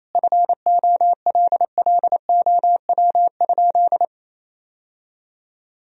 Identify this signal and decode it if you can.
Morse code, heard as a keyed tone, about 27 wpm, 700 Hz. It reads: FOLLOW?
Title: Sentences from Top 1000 Words - Review Part 3 - 28wpm